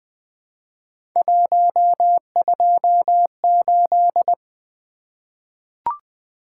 Morse code, heard as a keyed tone, about 20 wpm, 700 Hz. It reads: 128
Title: Three Letter-Number Combinations - 20wpm